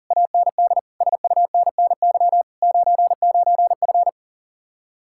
Morse code, heard as a keyed tone, about 49 wpm, 700 Hz. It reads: AND SUNNY 99F
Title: Speed-Racing Common QSO Phrases - 40wpm